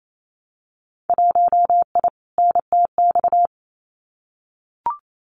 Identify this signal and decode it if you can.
Morse code, heard as a keyed tone, about 28 wpm, 700 Hz. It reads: 1I NTX
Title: Winter Field Day Exchanges Encore 28wpm